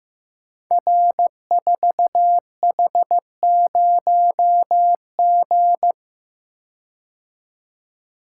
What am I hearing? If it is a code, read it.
Morse code, heard as a keyed tone, about 15 wpm, 700 Hz. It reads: R4H0G
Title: Code Groups - 5 Letter:Numbers - 15wpm